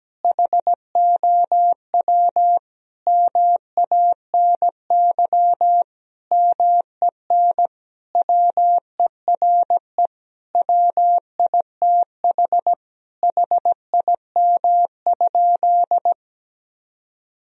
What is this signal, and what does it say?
Morse code, heard as a keyed tone, about 17 wpm, 700 Hz. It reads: HOW MANY MEN WERE WITH HIM?
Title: Sentences from Top 1000 Words - Review Part 1 - 17wpm